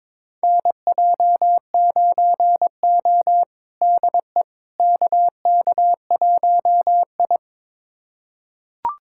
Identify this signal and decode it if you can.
Morse code, heard as a keyed tone, about 22 wpm, 700 Hz. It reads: NJ9O DE KK1I
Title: USA Callsign DE Callsign - 4 Characters - 22wpm